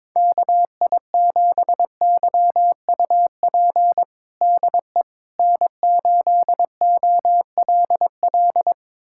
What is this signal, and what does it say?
Morse code, heard as a keyed tone, about 22 wpm, 700 Hz. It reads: KI7YUP DE N8OLL
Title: USA Callsign DE Callsign 22wpm